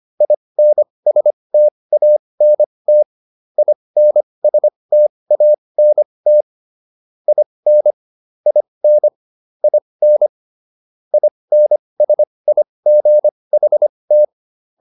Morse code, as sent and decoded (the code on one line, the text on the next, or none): .. -. ... - .- -. - / .. -. ... - .- -. - / .. -. / .. -. / .. -. / .. -. ... .. --. .... -
INSTANT INSTANT IN IN IN INSIGHT